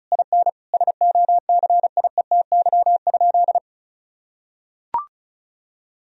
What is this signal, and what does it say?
Morse code, heard as a keyed tone, about 36 wpm, 700 Hz. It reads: IN SOCIETY?
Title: Sentences from Top 1000 Words - Review Part 4 - 35wpm